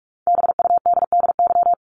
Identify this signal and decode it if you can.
Morse code, heard as a keyed tone, about 44 wpm, 700 Hz. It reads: BUDDY
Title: Top 1751-2000 Words 45wpm